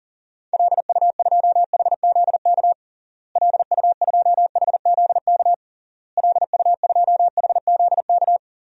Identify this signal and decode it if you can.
Morse code, heard as a keyed tone, about 40 wpm, 700 Hz. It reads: LU2HZK LU2HZK LU2HZK
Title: DXCC-100 Call Signs - 6 Characters - Repeated 3x - 40wpm